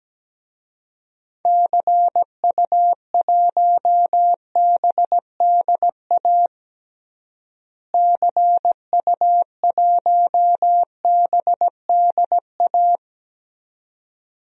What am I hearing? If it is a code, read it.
Morse code, heard as a keyed tone, about 17 wpm, 700 Hz. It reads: CU1BDA CU1BDA
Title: DXCC-100 Call Signs - 6 Characters - Repeated 2x - 17wpm